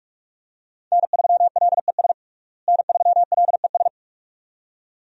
Morse code, heard as a keyed tone, about 47 wpm, 700 Hz. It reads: N3LES N3LES
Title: USA Call Signs - 5 Characters - Repeated 2x - 45wpm